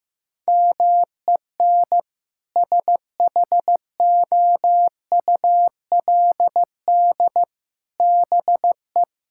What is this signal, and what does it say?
Morse code, heard as a keyed tone, about 15 wpm, 700 Hz. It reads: MEN SHOULD BE
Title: Sentences from Top 1600 Words 15wpm